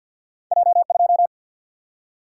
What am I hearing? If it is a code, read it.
Morse code, heard as a keyed tone, about 50 wpm, 700 Hz. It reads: J 2
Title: Single-Letter Single-Number 50wpm